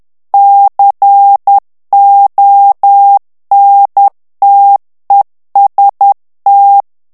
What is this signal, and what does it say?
Morse code, heard as a keyed tone, about 10 wpm, 800 Hz. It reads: CONTEST
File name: ITALIEN texte clair 2 vitesse 10 mots minute